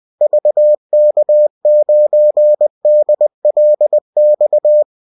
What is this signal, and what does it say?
Morse code, heard as a keyed tone, about 20 wpm, 600 Hz. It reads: VK9DLX